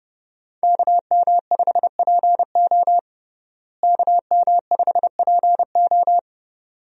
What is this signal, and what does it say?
Morse code, heard as a keyed tone, about 32 wpm, 700 Hz. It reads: KM5PO KM5PO
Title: USA Call Signs - 5 Characters - Repeated 2x - 30wpm